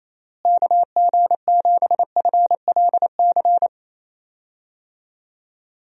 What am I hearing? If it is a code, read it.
Morse code, heard as a keyed tone, about 28 wpm, 700 Hz. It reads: KG7FLC
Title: USA Call Signs - 6 Characters - Repeated 2x - 28wpm